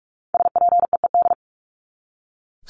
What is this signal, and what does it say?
Morse code, heard as a keyed tone, about 44 wpm, 700 Hz. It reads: SPEED
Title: Top 751-1000 Words 45wpm